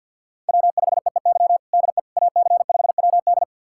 Morse code, heard as a keyed tone, about 52 wpm, 700 Hz. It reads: W5EEY DE AC5WD
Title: USA Callsign DE Callsign - 5 Characters - 50wpm